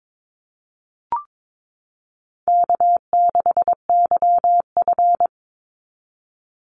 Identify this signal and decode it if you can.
Morse code, heard as a keyed tone, about 22 wpm, 700 Hz. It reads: K6YF
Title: Easy Call Signs - 22wpm